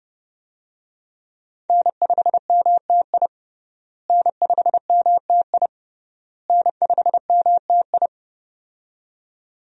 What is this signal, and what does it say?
Morse code, heard as a keyed tone, about 30 wpm, 700 Hz. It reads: N5MTI N5MTI N5MTI
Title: USA Call Signs - 5 Characters - Repeated 3x - 30wpm